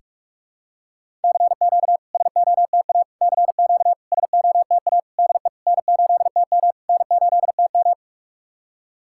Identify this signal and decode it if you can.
Morse code, heard as a keyed tone, about 46 wpm, 690 Hz. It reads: CQ SOTA CQ SOTA DE N8TM N8TM
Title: SOTA QSO Practice 45wpm